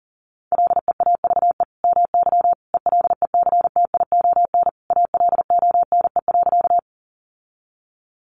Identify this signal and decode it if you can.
Morse code, heard as a keyed tone, about 37 wpm, 700 Hz. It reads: LEAVE MY ELECTION ALONE.
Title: Sentences from Top 1500 Words 40wpm